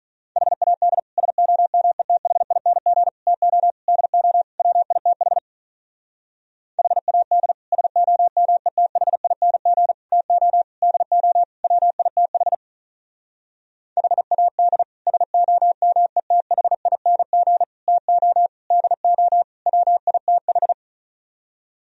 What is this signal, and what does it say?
Morse code, heard as a keyed tone, about 44 wpm, 700 Hz. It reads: HAD SOMETHING TO DO WITH HAD SOMETHING TO DO WITH HAD SOMETHING TO DO WITH